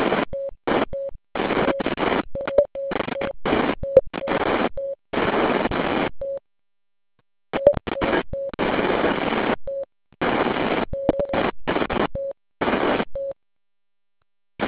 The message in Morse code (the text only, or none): none